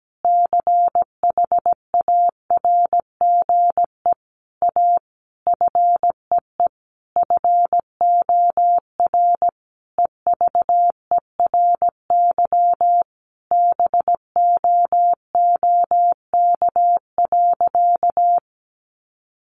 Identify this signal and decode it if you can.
Morse code, heard as a keyed tone, about 17 wpm, 700 Hz. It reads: CHARGE A FEE FOR EVERY BOOK.
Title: Sentences from Top 1500 Words 17wpm